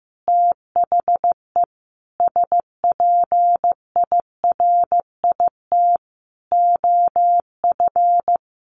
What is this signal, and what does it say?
Morse code, heard as a keyed tone, about 15 wpm, 700 Hz. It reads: THE SPIRIT OF
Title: Sentences from Top 1400 Words 15wpm